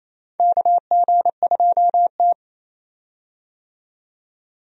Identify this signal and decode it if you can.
Morse code, heard as a keyed tone, about 28 wpm, 700 Hz. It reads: KG2T
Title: USA Call Signs - 4 Characters - 28wpm